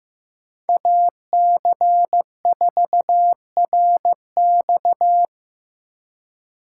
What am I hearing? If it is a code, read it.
Morse code, heard as a keyed tone, about 15 wpm, 700 Hz. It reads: AC4RX
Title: USA Call Signs - 5 Characters - 4x Character Spacing - 15wpm